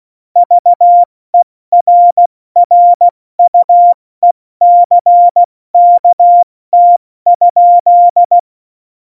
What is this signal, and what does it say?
Morse code, heard as a keyed tone, about 16 wpm, 700 Hz. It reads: VERRUECKT?
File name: Morsekurs / Witze / Witz 24. November 2020, 70 F80